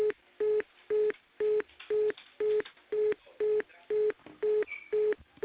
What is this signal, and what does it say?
Telephone call progress tone (425 Hz), heard as a telephone network's tone, about 420 Hz. Congestion tone.